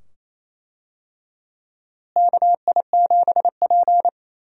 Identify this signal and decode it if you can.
Morse code, heard as a keyed tone, about 28 wpm, 700 Hz. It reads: KI7P
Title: USA Call Signs - 4 Characters - Repeated 2x - 28wpm